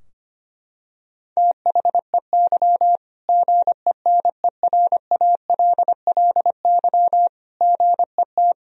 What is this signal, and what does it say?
Morse code, heard as a keyed tone, about 25 wpm, 700 Hz. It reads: THEY GENERALLY GET
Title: Sentences from Top 1000 Words - Review Part 4 - 25wpm